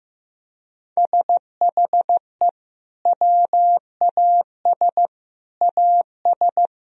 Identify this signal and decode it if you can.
Morse code, heard as a keyed tone, about 15 wpm, 700 Hz. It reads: SHE WAS AS
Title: Sentences from Top 1000 Words - Review Part 3 - 15wpm